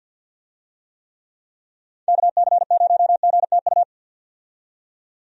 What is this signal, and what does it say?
Morse code, heard as a keyed tone, about 50 wpm, 700 Hz. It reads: KC0GTU
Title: USA Call Signs - 6 Characters - 50wpm